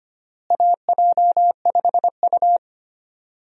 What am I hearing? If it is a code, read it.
Morse code, heard as a keyed tone, about 25 wpm, 700 Hz. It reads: AJ5U
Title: USA Call Signs - 4 Characters - Repeated 3x - 25wpm